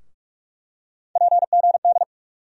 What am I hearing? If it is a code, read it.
Morse code, heard as a keyed tone, about 44 wpm, 700 Hz. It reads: PGD
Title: USA Call Sign Postfixes - 3 Letters 45wpm